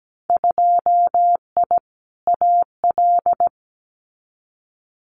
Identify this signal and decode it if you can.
Morse code, heard as a keyed tone, about 17 wpm, 700 Hz. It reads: 2I AL
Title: Winter Field Day Exchanges 17wpm